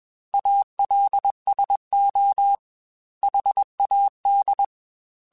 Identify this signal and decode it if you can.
Morse code, heard as a keyed tone, about 21 wpm, 800 Hz. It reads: ALSO HAD
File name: Weather Sets / Weather Sets-mp3s / Weather 5 5 @20